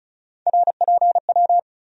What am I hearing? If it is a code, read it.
Morse code, heard as a keyed tone, about 34 wpm, 700 Hz. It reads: RPW
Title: USA Call Sign Postfixes - 3 Letters 35wpm